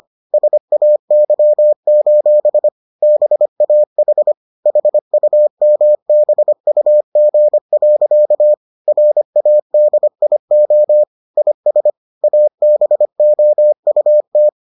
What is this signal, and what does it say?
Morse code, heard as a keyed tone, about 25 wpm, 600 Hz. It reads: SAY: BAH HUMBUG. RADIO IS ABOUT